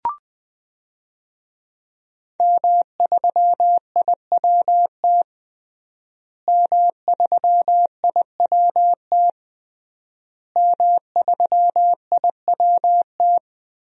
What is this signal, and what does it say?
Morse code, heard as a keyed tone, about 21 wpm, 700 Hz. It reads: M3IWT M3IWT M3IWT
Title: DXCC-100 Call Signs - 5 Characters - Repeated 3x - 20wpm